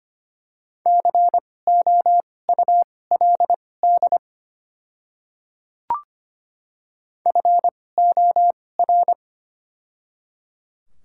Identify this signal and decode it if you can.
Morse code, heard as a keyed tone, about 25 wpm, spaced as for 18 wpm, 700 Hz. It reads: COULD FOR
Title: 2x Character Spacing - Top 200 Words - 25wpm